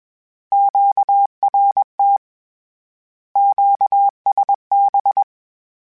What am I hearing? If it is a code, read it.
Morse code, heard as a keyed tone, about 21 wpm, 800 Hz. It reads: QRT QSB
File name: Misc / Misc-mp3s / Words @20